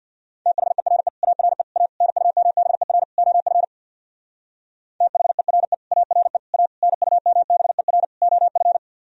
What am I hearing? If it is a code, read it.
Morse code, heard as a keyed tone, about 62 wpm, 700 Hz. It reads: THERE ARE A NUMBER OF THERE ARE A NUMBER OF
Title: Speed-Racing - Sets of 5 Words 50wpm